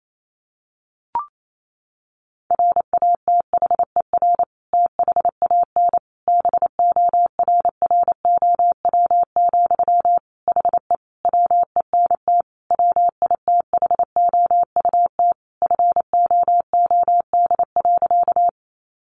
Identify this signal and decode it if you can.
Morse code, heard as a keyed tone, about 28 wpm, 700 Hz. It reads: RATHER THAN BORROW, HE WENT WITHOUT FOOD.
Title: Sentences from Top 1500 Words 28wpm